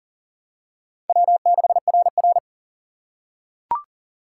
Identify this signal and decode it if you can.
Morse code, heard as a keyed tone, about 40 wpm, 700 Hz. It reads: W6RR
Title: USA Call Signs - 4 Characters - 2x Character Spacing - 40wpm